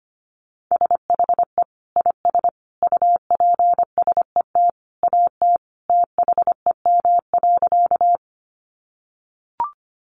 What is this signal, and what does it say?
Morse code, heard as a keyed tone, about 25 wpm, 700 Hz. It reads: SHE IS UPSET AT THEM.
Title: Sentences from Top 800 Words 25wpm